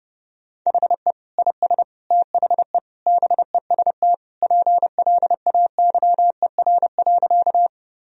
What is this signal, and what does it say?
Morse code, heard as a keyed tone, about 30 wpm, 700 Hz. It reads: HE IS THE BEST PLAYER.
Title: Sentences from Top 1000 Words - Review Part 4 - 30wpm